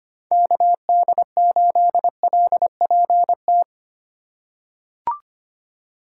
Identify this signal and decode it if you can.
Morse code, heard as a keyed tone, about 25 wpm, 700 Hz. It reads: KD8LPT
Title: USA Call Signs - 6 Characters - 4x Character Spacing - 25wpm